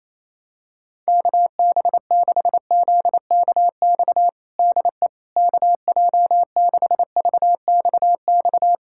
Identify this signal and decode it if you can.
Morse code, heard as a keyed tone, about 28 wpm, 700 Hz. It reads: KB6ZKX DE KJ6VXX
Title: USA Callsign DE Callsign - 6 Characters - 28wpm